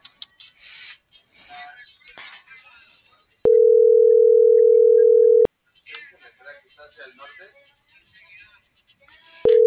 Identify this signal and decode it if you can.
Ringback tone (North America), 440 + 480 Hz.